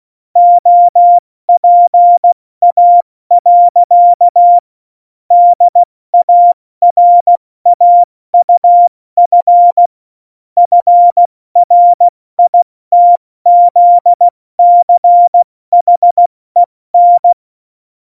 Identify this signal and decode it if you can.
Morse code, heard as a keyed tone, about 16 wpm, 700 Hz. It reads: OPA. DARAUF FRITZCHEN
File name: Morsekurs / Witze / Witz 12. September 17, Tempo 70